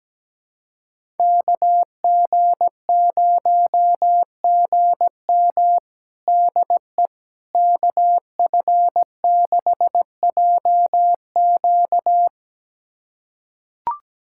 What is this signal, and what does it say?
Morse code, heard as a keyed tone, about 17 wpm, 700 Hz. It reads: KG0GM DE KF6JQ
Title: USA Callsign DE Callsign - 5 Characters - 17wpm